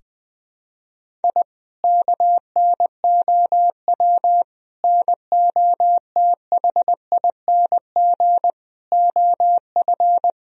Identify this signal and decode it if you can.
Morse code, heard as a keyed tone, about 20 wpm, 700 Hz. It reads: I KNOW NOTHING OF
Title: Sentences from Top 1000 Words - Review Part 3 - 20wpm